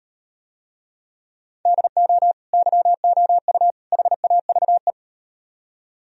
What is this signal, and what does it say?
Morse code, heard as a keyed tone, about 37 wpm, 700 Hz. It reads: DO YOU HAVE